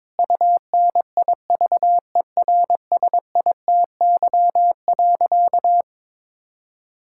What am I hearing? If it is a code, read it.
Morse code, heard as a keyed tone, about 22 wpm, 700 Hz. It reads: UNIVERSITY.
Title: Sentences from Top 1000 Words - Review Part 5 - 22wpm